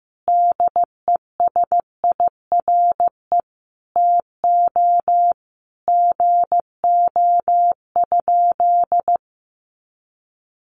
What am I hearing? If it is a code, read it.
Morse code, heard as a keyed tone, about 15 wpm, 700 Hz. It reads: DESIRE TO GO?
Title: Sentences from Top 1100 Words 15wpm